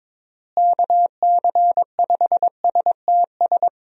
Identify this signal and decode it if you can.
Morse code, heard as a keyed tone, about 22 wpm, 700 Hz. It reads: KC5STS
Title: USA Call Signs - 6 Characters - Repeated 3x - 22wpm